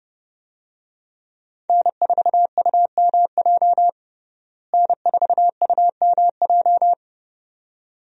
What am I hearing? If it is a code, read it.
Morse code, heard as a keyed tone, about 30 wpm, 700 Hz. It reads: N4UMJ N4UMJ
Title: USA Call Signs - 5 Characters - Repeated 2x - 30wpm